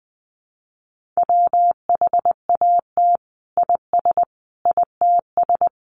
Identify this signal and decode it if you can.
Morse code, heard as a keyed tone, about 20 wpm, 700 Hz. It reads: WHAT IS ITS
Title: Sentences from Top 1400 Words 20wpm